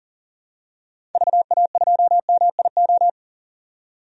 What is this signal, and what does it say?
Morse code, heard as a keyed tone, about 40 wpm, 700 Hz. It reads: VA2MIO
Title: DXCC-100 Call Signs - 6 Characters - 40wpm